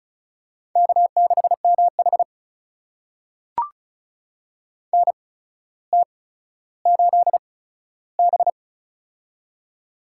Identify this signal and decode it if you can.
Morse code, heard as a keyed tone, about 36 wpm, spaced as for 15 wpm, 700 Hz. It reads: K6MH NT8B
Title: USA Call Signs - 4 Characters - 8x Character Spacing - 35wpm